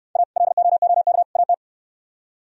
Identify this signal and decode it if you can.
Morse code, heard as a keyed tone, about 66 wpm, 700 Hz. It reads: A LOOK AT